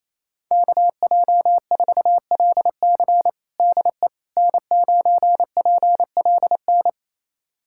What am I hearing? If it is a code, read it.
Morse code, heard as a keyed tone, about 28 wpm, 700 Hz. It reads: KJ4LC DE N9PLN
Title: USA Callsign DE Callsign - 5 Characters - 28wpm